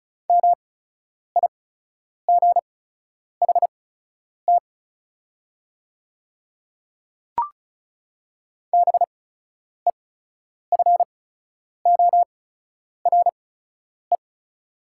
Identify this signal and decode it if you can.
Morse code, heard as a keyed tone, about 36 wpm, spaced as for 10 wpm, 700 Hz. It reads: MIGHT BEFORE
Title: 8x Character Spacing - Top 500 Words - 35wpm